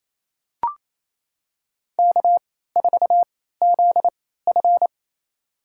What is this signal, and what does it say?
Morse code, heard as a keyed tone, about 28 wpm, spaced as for 16 wpm, 700 Hz. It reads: K4ZF
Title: USA Call Signs - 4 Characters - 3x Character Spacing - 28wpm